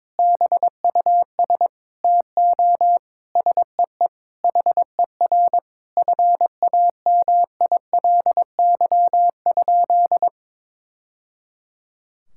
Morse code, heard as a keyed tone, about 22 wpm, 700 Hz. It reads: BUS TO SEE HER FAMILY?
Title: Sentences from Top 1000 Words - Review Part 6 - 22wpm